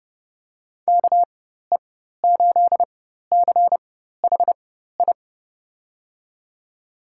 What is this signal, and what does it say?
Morse code, heard as a keyed tone, about 30 wpm, spaced as for 14 wpm, 700 Hz. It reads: KE8CHI
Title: USA Call Signs - 6 Characters - 4x Character Spacing - 30wpm